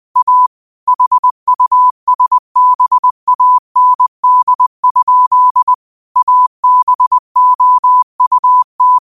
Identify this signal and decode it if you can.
Morse code, heard as a keyed tone, about 20 wpm, 1000 Hz. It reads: A HUSBAND? ABOUT
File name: qotdfiles / 20wpm